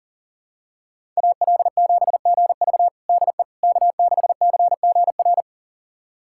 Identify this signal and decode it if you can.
Morse code, heard as a keyed tone, about 40 wpm, 700 Hz. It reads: AL7GV DE K6CGR
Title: USA Callsign DE Callsign - 5 Characters - 40wpm